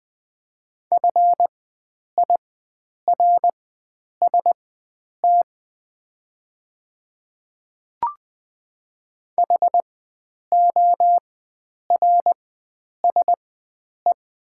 Morse code, heard as a keyed tone, about 20 wpm, spaced as for 9 wpm, 700 Hz. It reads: FIRST HORSE
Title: 4x Character Spacing - Top 500 Words - 20wpm